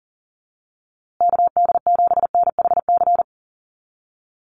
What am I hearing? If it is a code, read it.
Morse code, heard as a keyed tone, about 40 wpm, 700 Hz. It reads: KD7NHC
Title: USA Call Signs Repeated 3x 40wpm